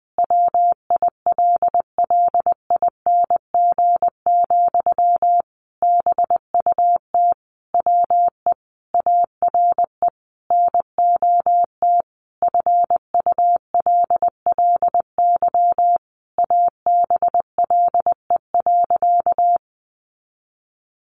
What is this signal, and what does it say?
Morse code, heard as a keyed tone, about 20 wpm, 700 Hz. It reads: WILLING, BUT WE ARE NOT FULLY ABLE.
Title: Sentences from Top 1300 Words 20wpm